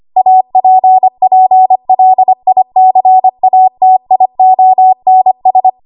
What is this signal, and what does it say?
Morse code, heard as a keyed tone, about 25 wpm, 750 Hz. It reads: APPLICATIONS
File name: Archive / 25 WPM / 25WPM